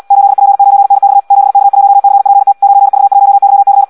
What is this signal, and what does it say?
Morse code, heard as a keyed tone, about 76 wpm, 800 Hz. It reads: 8P9NX 7K1CPT 9V1YC
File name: Audio practica CW / qrq-tx-78 2